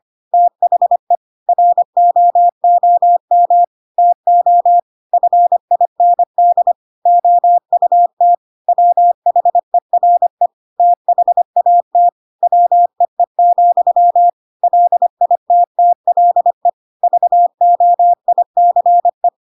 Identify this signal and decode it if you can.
Morse code, heard as a keyed tone, about 25 wpm, 700 Hz. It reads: THE ROOM TO FIND OUT WHERE THAT WEE, LITTLE VOICE
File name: Pinocchio / Pinocchio ch1 25 25 ik1hge